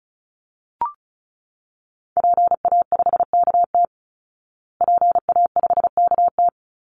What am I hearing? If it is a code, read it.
Morse code, heard as a keyed tone, about 34 wpm, 700 Hz. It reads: PA5KT PA5KT
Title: Call Signs Repeated 3x - 35wpm